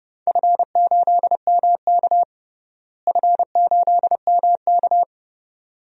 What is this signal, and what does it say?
Morse code, heard as a keyed tone, about 30 wpm, 700 Hz. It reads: F8MK F8MK
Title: DXCC-100 Call Signs - 4 Characters - Repeated 2x - 30wpm